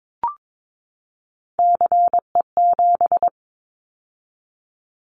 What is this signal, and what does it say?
Morse code, heard as a keyed tone, about 22 wpm, 700 Hz. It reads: CE7
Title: Callsign Prefixes for Top 100 DXCC - 22wpm